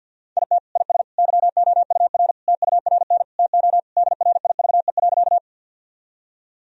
Morse code, heard as a keyed tone, about 50 wpm, 700 Hz. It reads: IT IS YOUR TURN TO DRIVE.
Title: Sentences from Top 1000 Words - Review Part 2 - 50wpm